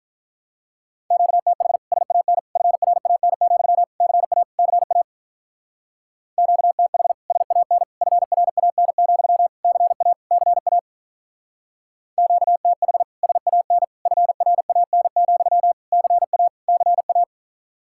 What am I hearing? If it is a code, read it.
Morse code, heard as a keyed tone, about 43 wpm, 700 Hz. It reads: QTH SAN FRAN, CA CA QTH SAN FRAN, CA CA QTH SAN FRAN, CA CA
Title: Speed-Racing Common QSO Phrases - 35wpm